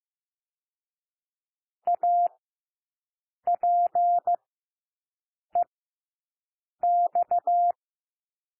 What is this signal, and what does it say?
Morse code, heard as a keyed tone, about 15 wpm, spaced as for 6 wpm, 700 Hz. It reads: APEX
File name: cwmp3 / W1 SOUND WORDS 15 wpm